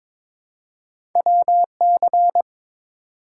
Morse code, heard as a keyed tone, about 23 wpm, 700 Hz. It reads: WC
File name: practice / 22wpm / Two Letter-Number Combinations - 22wpm